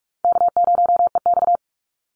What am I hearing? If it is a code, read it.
Morse code, heard as a keyed tone, about 44 wpm, 700 Hz. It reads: K0EX
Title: Easy Call Signs Repeated 2x - 45wpm